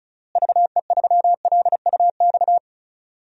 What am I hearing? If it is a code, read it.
Morse code, heard as a keyed tone, about 36 wpm, 700 Hz. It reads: VE3LUX
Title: DXCC-100 Call Signs - 6 Characters - Repeated 3x - 35wpm